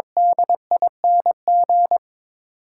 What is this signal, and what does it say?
Morse code, heard as a keyed tone, about 21 wpm, 700 Hz. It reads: DING
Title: Sentences from Top 1100 Words - New Word Drill Alt - 22wpm